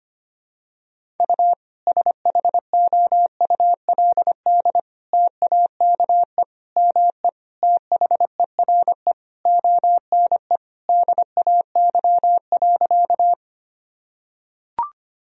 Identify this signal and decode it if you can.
Morse code, heard as a keyed tone, about 25 wpm, 700 Hz. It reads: U SHOULD TAKE ME THERE ONE DAY.
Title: Sentences from Top 300 Words and Abbreviations 25wpm